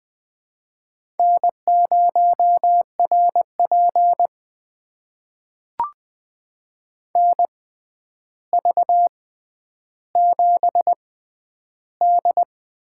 Morse code, heard as a keyed tone, about 20 wpm, spaced as for 10 wpm, 700 Hz. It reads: N0RP NV7D
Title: USA Call Signs - 4 Characters - 6x Character Spacing - 20wpm